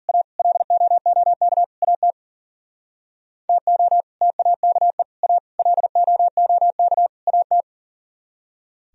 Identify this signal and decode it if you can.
Morse code, heard as a keyed tone, about 51 wpm, 700 Hz. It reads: A LOOK AT TO TAKE A LOOK AT